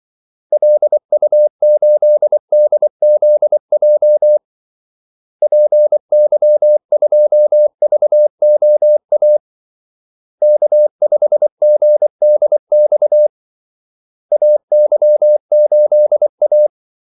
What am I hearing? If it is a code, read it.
Morse code, heard as a keyed tone, about 24 wpm, 600 Hz. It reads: LU8DZJ PY2VOA K5GDX AY8A
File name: Callsigns-24wpm